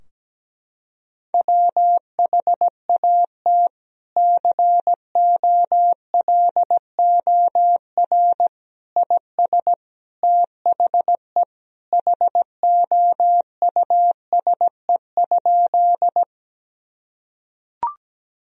Morse code, heard as a keyed tone, about 17 wpm, 700 Hz. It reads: WHAT COLOR IS THE HOUSE?
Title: Sentences from Top 1000 Words - Review Part 2 - 17wpm